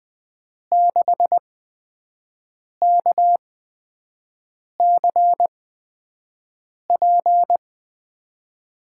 Morse code, heard as a keyed tone, about 20 wpm, spaced as for 5 wpm, 700 Hz. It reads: 6KCP
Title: USA Call Signs - 6 Characters - 8x Character Spacing - 20wpm